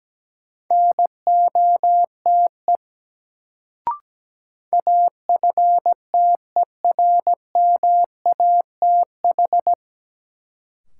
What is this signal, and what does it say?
Morse code, heard as a keyed tone, about 17 wpm, 700 Hz. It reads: NOTE AFTERMATH
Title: Sets of 5 Words - New Word Drill - 17wpm